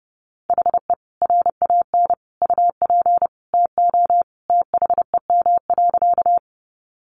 Morse code, heard as a keyed tone, about 30 wpm, 700 Hz. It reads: HE RAN UP TO THEM.
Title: Sentences from Top 500 Words 30wpm